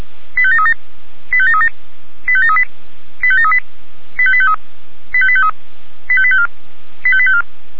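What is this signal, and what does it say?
Five-tone selective call (ZVEI), heard as quick runs of five tones, tones of about 80 ms. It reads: 85737 85737 85738 85738 85853 85853 85854 85854
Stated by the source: ZVEI five-tone call 85719 received by SDR.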